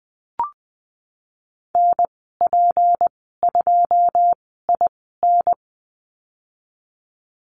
Morse code, heard as a keyed tone, about 20 wpm, spaced as for 15 wpm, 700 Hz. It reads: NP2IN
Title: 167 Lesson 27g - Number 7 - Call Signs 20wpm